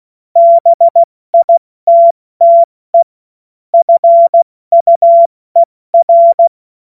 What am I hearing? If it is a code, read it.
Morse code, heard as a keyed tone, about 16 wpm, 680 Hz. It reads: BITTE FUER
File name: Morsekurs / Witze / Witz 06 Februar 2024 70 F80